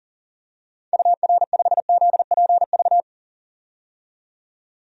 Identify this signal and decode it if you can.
Morse code, heard as a keyed tone, about 40 wpm, 700 Hz. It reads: UR5ZPV
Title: DXCC-100 Call Signs - 6 Characters - 40wpm